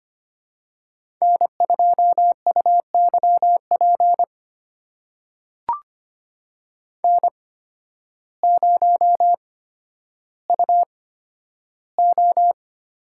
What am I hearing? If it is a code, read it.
Morse code, heard as a keyed tone, about 25 wpm, spaced as for 12 wpm, 700 Hz. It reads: N2UYP N0UO
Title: USA Call Signs - 8x Character Spacing - 25wpm